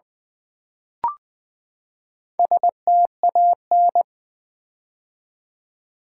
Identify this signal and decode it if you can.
Morse code, heard as a keyed tone, about 20 wpm, 700 Hz. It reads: STAN
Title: Build A Word - 4 Letters At A Time - 8-Letter Words - 20wpm